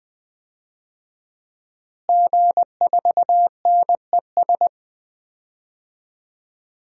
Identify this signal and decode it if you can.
Morse code, heard as a keyed tone, about 20 wpm, 700 Hz. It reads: G4NES
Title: DXCC-100 Call Signs - 5 Characters - Repeated 2x - 20wpm